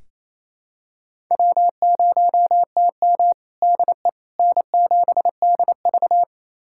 Morse code, heard as a keyed tone, about 28 wpm, 700 Hz. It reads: W0TM DE N7DV
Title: USA Callsign DE Callsign - 4 Characters - 28wpm